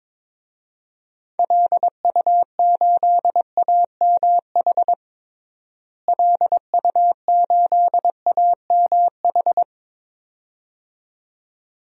Morse code, heard as a keyed tone, about 22 wpm, 700 Hz. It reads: LU8AMH LU8AMH
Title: DXCC-100 Call Signs - 6 Characters - Repeated 2x - 22wpm